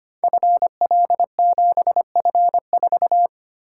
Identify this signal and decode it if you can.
Morse code, heard as a keyed tone, about 25 wpm, 700 Hz. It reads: FL7F4
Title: Code Groups - 5 Letter:Numbers - 25wpm